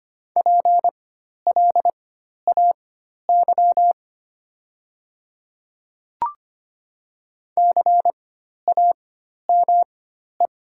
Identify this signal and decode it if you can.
Morse code, heard as a keyed tone, about 25 wpm, spaced as for 12 wpm, 700 Hz. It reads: PLAY CAME
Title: 4x Character Spacing - Top 300 Words - 25wpm